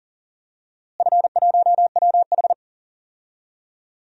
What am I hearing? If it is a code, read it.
Morse code, heard as a keyed tone, about 40 wpm, 700 Hz. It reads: F1WH
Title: DXCC-100 Call Signs - 4 Characters - Repeated 3x - 40wpm